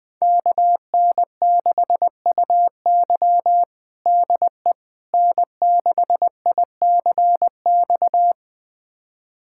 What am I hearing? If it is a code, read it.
Morse code, heard as a keyed tone, about 20 wpm, 700 Hz. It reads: KN6UY DE N6ICX
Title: USA Callsign DE Callsign - 5 Characters - 20wpm